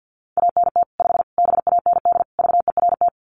Morse code, heard as a keyed tone, about 50 wpm, 700 Hz. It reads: ANT 5 BAND VERT
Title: Common QSO Phrases 50wpm